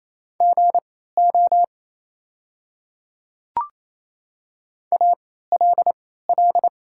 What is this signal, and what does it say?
Morse code, heard as a keyed tone, about 28 wpm, spaced as for 16 wpm, 690 Hz. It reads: GO ALL
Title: 3x Character Spacing - Top 100 Words - 28wpm